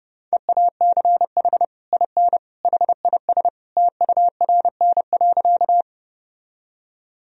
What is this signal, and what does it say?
Morse code, heard as a keyed tone, about 30 wpm, 700 Hz. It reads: EACH IN HIS TURN.
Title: Sentences from Top 1000 Words - Review Part 1 - 30wpm